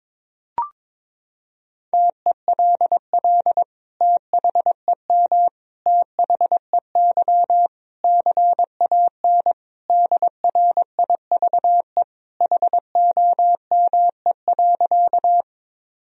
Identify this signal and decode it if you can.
Morse code, heard as a keyed tone, about 22 wpm, 700 Hz. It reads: TELL THEM THEY CAN DRIVE HOME.
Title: Sentences from Top 1000 Words - Review Part 2 - 22wpm